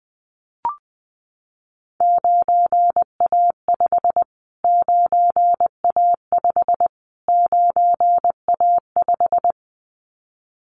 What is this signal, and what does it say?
Morse code, heard as a keyed tone, about 20 wpm, 700 Hz. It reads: 9A5 9A5 9A5
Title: Callsign Prefixes for Top 25 DXCC - Repeated 3x - 20wpm